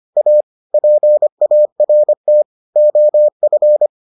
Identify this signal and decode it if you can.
Morse code, heard as a keyed tone, about 25 wpm, 600 Hz. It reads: A PART OF